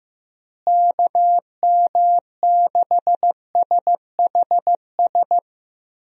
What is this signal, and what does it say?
Morse code, heard as a keyed tone, about 15 wpm, 700 Hz. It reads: KM6SHS
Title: USA Call Signs - 6 Characters - Repeated 2x - 15wpm